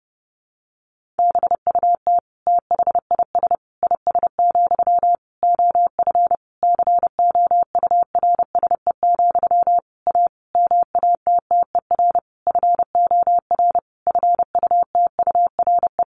Morse code, heard as a keyed tone, about 30 wpm, 700 Hz. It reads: BUT THIS IS, OF COURSE, A MATTER FOR FUTURE
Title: Sentences from Top 1500 Words 30wpm